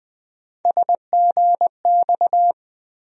Morse code, heard as a keyed tone, about 20 wpm, 700 Hz. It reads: SGX
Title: USA Call Sign Postfixes - 3 Letters 20wpm